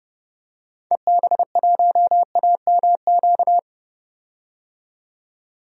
Morse code, heard as a keyed tone, about 30 wpm, 700 Hz. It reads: EB1AMQ
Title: DXCC-100 Call Signs - 6 Characters - 30wpm